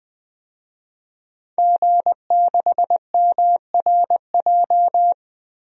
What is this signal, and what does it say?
Morse code, heard as a keyed tone, about 20 wpm, 700 Hz. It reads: G6MRJ
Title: DXCC-100 Call Signs - 5 Characters - Repeated 3x - 20wpm